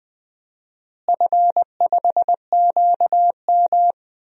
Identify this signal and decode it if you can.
Morse code, heard as a keyed tone, about 20 wpm, 700 Hz. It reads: F5QM
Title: DXCC-100 Call Signs - 4 Characters - Repeated 3x - 20wpm